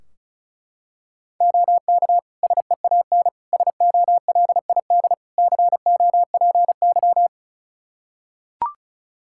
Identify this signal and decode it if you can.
Morse code, heard as a keyed tone, about 34 wpm, 700 Hz. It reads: OK SEAN SOLID COPY
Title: Practice QSO - Exchange 3 - More Info - 35wpm